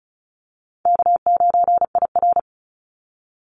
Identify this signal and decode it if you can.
Morse code, heard as a keyed tone, about 34 wpm, 700 Hz. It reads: K9IR
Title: Call Signs - 35wpm